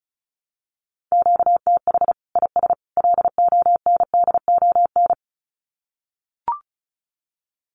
Morse code, heard as a keyed tone, about 34 wpm, 700 Hz. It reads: QTH IS LONDON
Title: Common QSO Phrases 35wpm